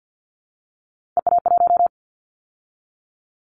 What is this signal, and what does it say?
Morse code, heard as a keyed tone, about 50 wpm, 700 Hz. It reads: EA1
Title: Callsign Prefixes for Top 25 DXCC - 50wpm